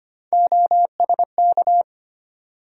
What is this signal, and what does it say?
Morse code, heard as a keyed tone, about 25 wpm, 700 Hz. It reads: OSK
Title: USA Call Sign Postfixes - 3 Letters 25wpm